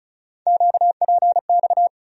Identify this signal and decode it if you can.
Morse code, heard as a keyed tone, about 34 wpm, 700 Hz. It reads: QPX
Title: USA Call Sign Postfixes - 3 Letters 35wpm